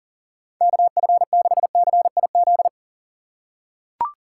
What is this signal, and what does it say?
Morse code, heard as a keyed tone, about 40 wpm, 700 Hz. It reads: KF6CIZ
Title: USA Call Signs - 6 Characters - 3x Character Spacing - 40wpm